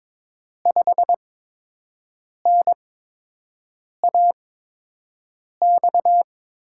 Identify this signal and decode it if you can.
Morse code, heard as a keyed tone, about 22 wpm, spaced as for 6 wpm, 700 Hz. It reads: 5NAX
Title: USA Call Signs - 6 Characters - 8x Character Spacing - 22wpm